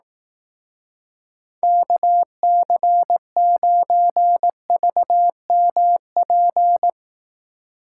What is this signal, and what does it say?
Morse code, heard as a keyed tone, about 18 wpm, 700 Hz. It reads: KC9VMP
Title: Speed Racing - USA 2x3 Call Signs 15wpm